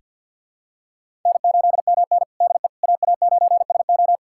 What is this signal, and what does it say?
Morse code, heard as a keyed tone, about 46 wpm, 700 Hz. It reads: N8MN DE AA9SO
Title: USA Callsign DE Callsign 50wpm